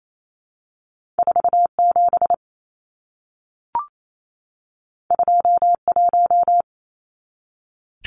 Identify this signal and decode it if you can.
Morse code, heard as a keyed tone, about 28 wpm, 700 Hz. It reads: 47 21
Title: Numbers - 2-Digit 28wpm